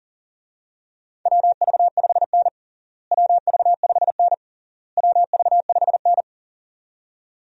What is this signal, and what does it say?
Morse code, heard as a keyed tone, about 37 wpm, 700 Hz. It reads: WV5N WV5N WV5N
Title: USA Call Signs - 4 Characters - Repeated 3x - 40wpm